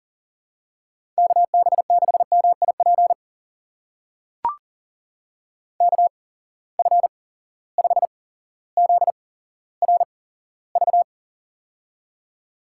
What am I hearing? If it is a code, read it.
Morse code, heard as a keyed tone, about 38 wpm, spaced as for 17 wpm, 700 Hz. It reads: KB6MIP KF5ZRV
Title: USA Call Signs - 8x Character Spacing - 40wpm